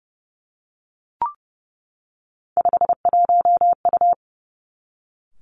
Morse code, heard as a keyed tone, about 31 wpm, 700 Hz. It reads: 51U
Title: Three Challenging Letter-Number Combinations 30wpm